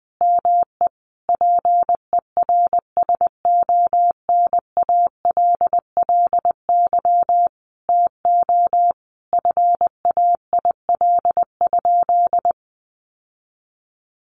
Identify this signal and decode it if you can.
Morse code, heard as a keyed tone, about 20 wpm, 700 Hz. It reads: ME PERSONALLY TO FAIL?
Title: Sentences from Top 1400 Words 20wpm